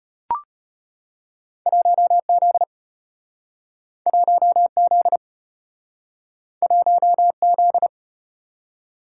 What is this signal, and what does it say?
Morse code, heard as a keyed tone, about 35 wpm, 700 Hz. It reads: 1Z 1Z 1Z